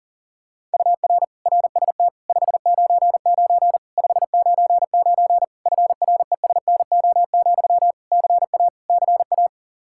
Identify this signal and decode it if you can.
Morse code, heard as a keyed tone, about 40 wpm, 700 Hz. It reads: UR RST 599 599 FRESNO, CA CA
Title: Practice QSO - Exchange 1 - Beginning - 40wpm